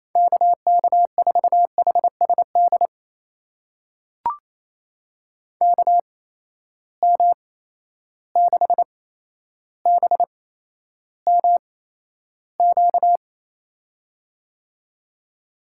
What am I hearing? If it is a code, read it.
Morse code, heard as a keyed tone, about 28 wpm, spaced as for 12 wpm, 700 Hz. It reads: KK4HSD KM6BMQ
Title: USA Call Signs - 8x Character Spacing - 28wpm